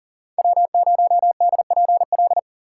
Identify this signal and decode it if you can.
Morse code, heard as a keyed tone, about 40 wpm, 700 Hz. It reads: W0DPL
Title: USA Call Signs - 5 Characters - 3x Character Spacing - 40wpm